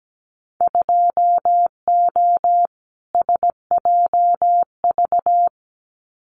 Morse code, heard as a keyed tone, about 17 wpm, 700 Hz. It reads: 2O SJV
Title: Winter Field Day Exchanges Encore 17wpm